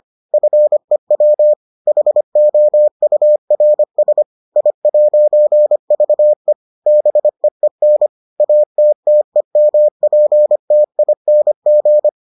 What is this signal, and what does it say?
Morse code, heard as a keyed tone, about 25 wpm, 600 Hz. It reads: FEW HOURS I'VE BEEN ATTEMPTING